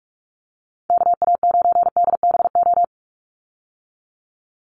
Morse code, heard as a keyed tone, about 44 wpm, 690 Hz. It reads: KA9DBO
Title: USA Call Signs 45wpm